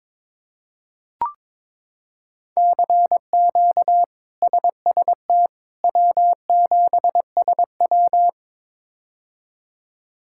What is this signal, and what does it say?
Morse code, heard as a keyed tone, about 22 wpm, 700 Hz. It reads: CQ SST W7SW
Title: K1USN SST QSO Practice - 22wpm